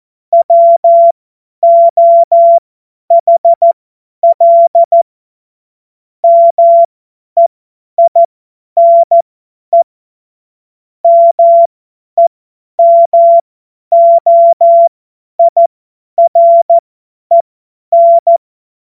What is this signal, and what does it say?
Morse code, heard as a keyed tone, about 14 wpm, spaced as for 10 wpm, 680 Hz. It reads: WOHL MEINE MEMOIREN
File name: Morsekurs / Witze / Witz 27. Dezember 2022 50 F70